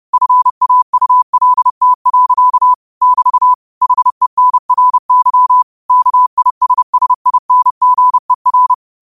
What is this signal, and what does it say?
Morse code, heard as a keyed tone, about 30 wpm, 1000 Hz. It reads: FAULT. <BT> HENRY KISSINGER
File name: qotdfiles / 30wpm